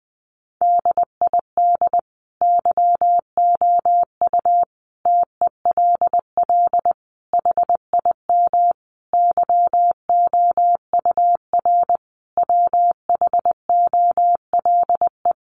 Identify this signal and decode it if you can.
Morse code, heard as a keyed tone, about 20 wpm, 700 Hz. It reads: DID YOU TELL HIM YOUR WHOLE
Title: Sentences from Top 400 Words 20wpm